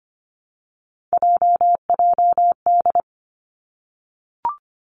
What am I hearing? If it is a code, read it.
Morse code, heard as a keyed tone, about 25 wpm, 700 Hz. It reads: JJD
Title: Three Challenging Letter-Number Combinations 25wpm